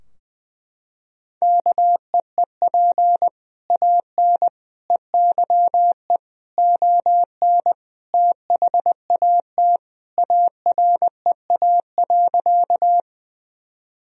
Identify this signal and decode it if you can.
Morse code, heard as a keyed tone, about 21 wpm, 700 Hz. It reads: KEEP AN EYE ON THAT AREA.
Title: Sentences from Top 1000 Words - Review Part 2 - 20wpm